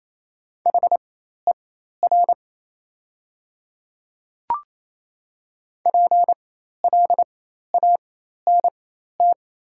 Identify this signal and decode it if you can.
Morse code, heard as a keyed tone, about 28 wpm, spaced as for 13 wpm, 700 Hz. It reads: HER PLANT
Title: 4x Character Spacing - Top 500 Words - 28wpm